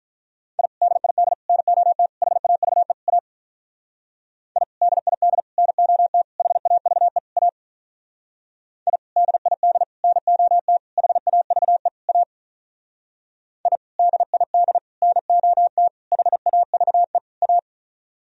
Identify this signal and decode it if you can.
Morse code, heard as a keyed tone, about 43 wpm, 700 Hz. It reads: I DID NOT HAVE A I DID NOT HAVE A I DID NOT HAVE A I DID NOT HAVE A
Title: Speed-Racing - Sets of 5 Words 35wpm